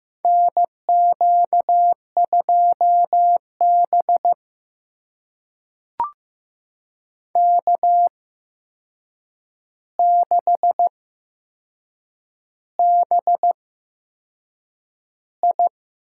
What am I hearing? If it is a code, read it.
Morse code, heard as a keyed tone, about 16 wpm, spaced as for 6 wpm, 700 Hz. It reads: NQ2B K6BI
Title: USA Call Signs - 4 Characters - 8x Character Spacing - 15wpm